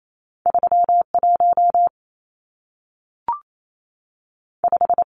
Morse code, heard as a keyed tone, about 28 wpm, 700 Hz. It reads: 31 5
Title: Numbers - 2-Digit 28wpm